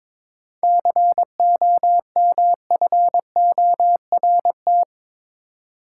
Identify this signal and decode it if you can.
Morse code, heard as a keyed tone, about 22 wpm, 700 Hz. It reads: COMFORT
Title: Seven Letter Words - 22wpm